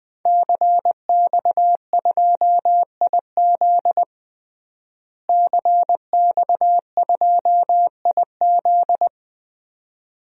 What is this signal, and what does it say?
Morse code, heard as a keyed tone, about 20 wpm, 700 Hz. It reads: CX2IZ CX2IZ
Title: DXCC-100 Call Signs - 5 Characters - Repeated 2x - 20wpm